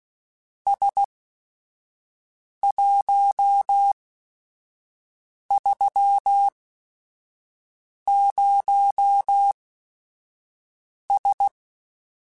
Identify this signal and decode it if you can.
Morse code, heard as a keyed tone, about 16 wpm, spaced as for 5 wpm, 800 Hz. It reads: S130S
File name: QSOs / QSOs-mp3s / WD0BGP 1@5